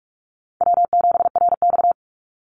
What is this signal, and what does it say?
Morse code, heard as a keyed tone, about 44 wpm, 700 Hz. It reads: W7RX
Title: Easy Call Signs Repeated 3x - 45wpm